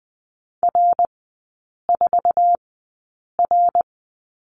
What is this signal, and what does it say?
Morse code, heard as a keyed tone, about 20 wpm, spaced as for 8 wpm, 700 Hz. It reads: R4R
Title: 107 Lesson 18e - Letter m - Three Characters 20wpm